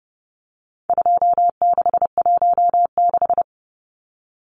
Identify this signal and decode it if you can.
Morse code, heard as a keyed tone, about 30 wpm, 700 Hz. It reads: 2616
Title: Numbers - 4-Digit 30wpm